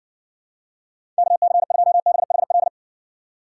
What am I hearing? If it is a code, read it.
Morse code, heard as a keyed tone, about 60 wpm, 700 Hz. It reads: KC2BHL